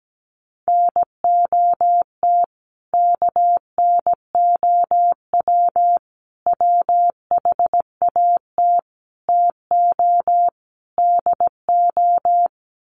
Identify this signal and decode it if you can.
Morse code, heard as a keyed tone, about 17 wpm, 700 Hz. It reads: NOT KNOW WHAT TO DO
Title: Sets of 5 Words 17wpm